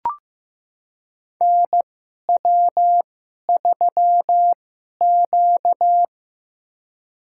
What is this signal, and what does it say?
Morse code, heard as a keyed tone, about 15 wpm, spaced as for 11 wpm, 700 Hz. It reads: NW3Q
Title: USA Call Signs - 4 Characters - 2x Character Spacing - 15wpm